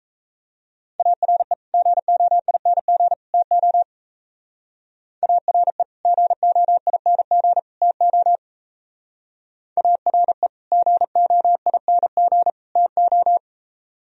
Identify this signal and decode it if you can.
Morse code, heard as a keyed tone, about 34 wpm, 700 Hz. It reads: ARE GOING TO ARE GOING TO ARE GOING TO
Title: Speed-Racing - Sets of 3 Words 28wpm